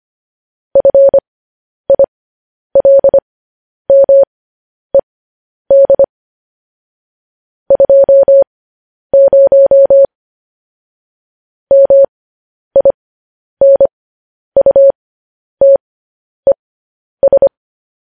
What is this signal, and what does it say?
Morse code, heard as a keyed tone, about 25 wpm, spaced as for 10 wpm, 550 Hz. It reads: FILMED 20 MINUTES
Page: news-headlines-in-morse-code-at-10-wpm